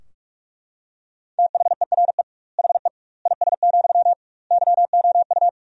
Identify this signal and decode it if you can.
Morse code, heard as a keyed tone, about 41 wpm, 700 Hz. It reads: THERE HE IS, YOU
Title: 2x Word-Spacing - Sentences from Top 100 Words - 45wpm